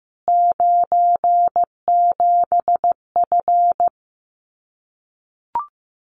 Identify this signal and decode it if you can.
Morse code, heard as a keyed tone, about 15 wpm, 700 Hz. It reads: 97F
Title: Common QSO Phrases Encore 15wpm